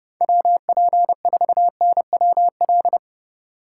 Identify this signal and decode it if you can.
Morse code, heard as a keyed tone, about 29 wpm, 700 Hz. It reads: WP4NWL